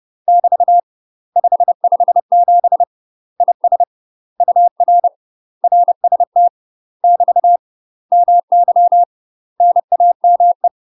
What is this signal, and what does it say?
Morse code, heard as a keyed tone, about 30 wpm, 700 Hz. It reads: <BT> 557 IS UR RST <BT> MY NAME
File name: UK MASTER QSOs TRAINING / 30WPM / UK MASTER QSOs 0034